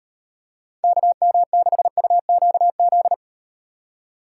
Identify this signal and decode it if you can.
Morse code, heard as a keyed tone, about 38 wpm, 700 Hz. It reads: KM6UQZ